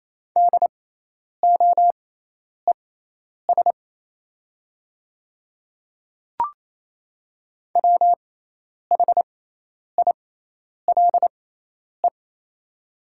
Morse code, heard as a keyed tone, about 28 wpm, spaced as for 10 wpm, 700 Hz. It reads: DOES WHILE
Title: 6x Character Spacing - Top 500 Words - 28wpm